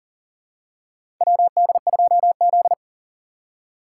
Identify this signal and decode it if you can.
Morse code, heard as a keyed tone, about 40 wpm, 700 Hz. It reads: WD2Z
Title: USA Call Signs - 4x Character Spacing - 40wpm